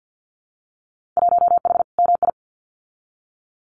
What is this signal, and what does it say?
Morse code, heard as a keyed tone, about 50 wpm, 700 Hz. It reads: 1H MI
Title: Winter Field Day Exchanges 50wpm